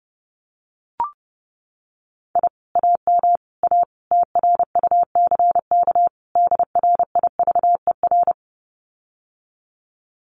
Morse code, heard as a keyed tone, about 30 wpm, 700 Hz. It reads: I AM A TRUCK DRIVER
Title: Common QSO Phrases Encore 30wpm